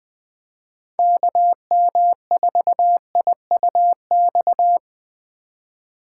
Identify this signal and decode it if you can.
Morse code, heard as a keyed tone, about 21 wpm, 700 Hz. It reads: KM4IUX
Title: Speed Racing - USA 2x3 Call Signs 17wpm